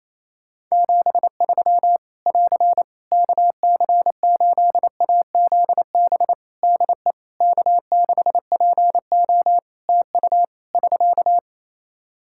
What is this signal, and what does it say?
Morse code, heard as a keyed tone, about 28 wpm, 700 Hz. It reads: 73 <AR> KC8AZB DE K6PO TU <SK>
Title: Practice QSO - Exchange 4 - Ending - 28wpm